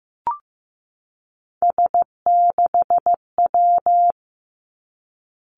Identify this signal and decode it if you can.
Morse code, heard as a keyed tone, about 15 wpm, 700 Hz. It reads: S6W
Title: Three Challenging Letter-Number Combinations 15wpm